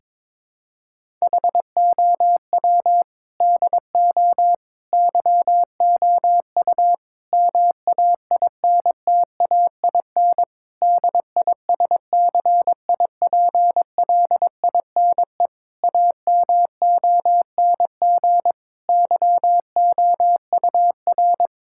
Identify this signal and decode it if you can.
Morse code, heard as a keyed tone, about 22 wpm, 700 Hz. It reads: HOW DO YOU MAINTAIN DISCIPLINE AMONG YOUR
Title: Sentences from Top 1000 Words - Review Part 6 - 22wpm